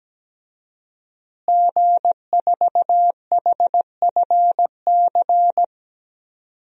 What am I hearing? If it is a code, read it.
Morse code, heard as a keyed tone, about 17 wpm, 700 Hz. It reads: G4HFC
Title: DXCC-100 Call Signs - 5 Characters - Repeated 3x - 17wpm